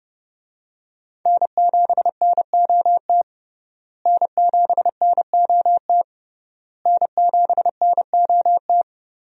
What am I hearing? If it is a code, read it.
Morse code, heard as a keyed tone, about 30 wpm, 700 Hz. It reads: N7NOT N7NOT N7NOT
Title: USA Call Signs - 5 Characters - Repeated 3x - 30wpm